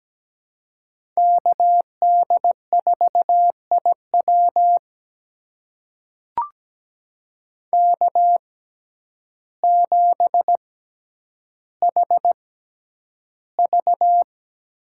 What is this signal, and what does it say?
Morse code, heard as a keyed tone, about 17 wpm, spaced as for 9 wpm, 700 Hz. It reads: KD4IW K7HV
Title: USA Call Signs - 5 Characters - 6x Character Spacing - 17wpm